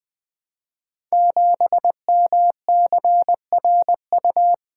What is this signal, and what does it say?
Morse code, heard as a keyed tone, about 20 wpm, 700 Hz. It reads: 7MCRU
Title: Code Groups - 5 Letter:Numbers - 20wpm